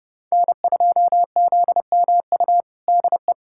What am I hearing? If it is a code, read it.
Morse code, heard as a keyed tone, about 32 wpm, 700 Hz. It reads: N2ZMU DE
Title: USA Callsign DE Callsign 30wpm